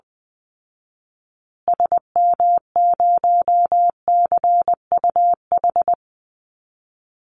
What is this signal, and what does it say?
Morse code, heard as a keyed tone, about 20 wpm, 700 Hz. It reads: SM0CUH
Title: Call Signs Repeated 3x - 20wpm